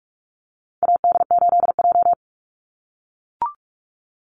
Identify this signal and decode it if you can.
Morse code, heard as a keyed tone, about 44 wpm, 700 Hz. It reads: AD8J
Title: Call Signs - 45wpm